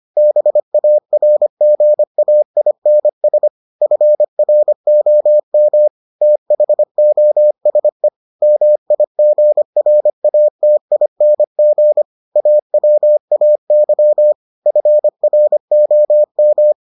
Morse code, heard as a keyed tone, about 25 wpm, 600 Hz. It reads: BARGAINS FROM THOSE MIGRATING AWAY FROM